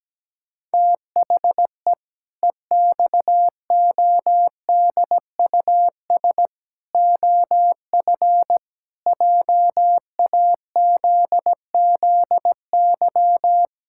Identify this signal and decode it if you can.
Morse code, heard as a keyed tone, about 17 wpm, 700 Hz. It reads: THE EXODUS OF JAZZY
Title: Pangram Challenge v2 - 17wpm